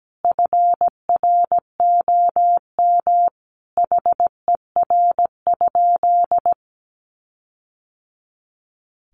Morse code, heard as a keyed tone, about 17 wpm, 700 Hz. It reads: FROM HER?
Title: Sentences from Top 1500 Words 17wpm